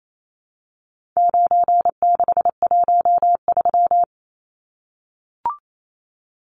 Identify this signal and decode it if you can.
Morse code, heard as a keyed tone, about 28 wpm, 700 Hz. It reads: 9613
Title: Numbers - 4-Digit 28wpm